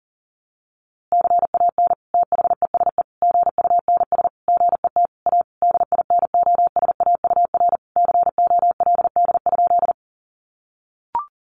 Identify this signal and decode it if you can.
Morse code, heard as a keyed tone, about 37 wpm, 700 Hz. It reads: CAN THESE GUNS GET A DINOSAUR COLD?
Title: A Sound Of Thunder 40wpm